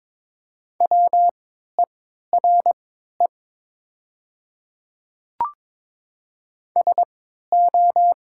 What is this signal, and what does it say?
Morse code, heard as a keyed tone, about 22 wpm, spaced as for 12 wpm, 700 Hz. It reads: WERE SO
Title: 3x Character Spacing - Top 100 Words - 22wpm